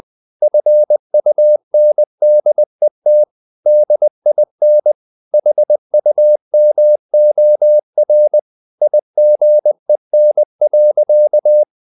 Morse code, heard as a keyed tone, about 20 wpm, 600 Hz. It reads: FUNDET DIN HUMOR IGEN.
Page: Ferie 20 wpm